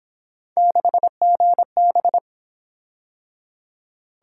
Morse code, heard as a keyed tone, about 27 wpm, 700 Hz. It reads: 6GB
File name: Speed-Racing - Three Letter-Number Combinations - 22wpm